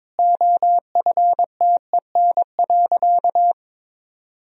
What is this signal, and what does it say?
Morse code, heard as a keyed tone, about 21 wpm, 700 Hz. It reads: OFTEN.
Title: Sentences from Top 1000 Words - Review Part 2 - 22wpm